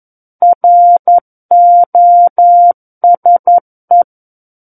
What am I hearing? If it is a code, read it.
Morse code, heard as a keyed tone, about 11 wpm, 700 Hz. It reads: ROSE
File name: Book One / 02-Chapter 2 - 11 WPM